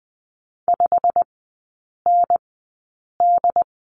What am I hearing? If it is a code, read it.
Morse code, heard as a keyed tone, about 20 wpm, spaced as for 8 wpm, 700 Hz. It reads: 5ND
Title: 100 Lesson 17e - Letter u - Three Characters 20wpm